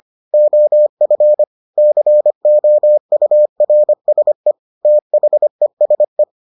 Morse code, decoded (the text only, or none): OF COURSE THESE